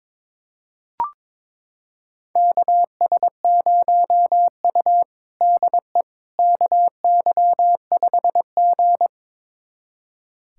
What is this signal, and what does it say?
Morse code, heard as a keyed tone, about 22 wpm, 700 Hz. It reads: KS0U DE KY5G
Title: USA Callsign DE Callsign - 4 Characters - 22wpm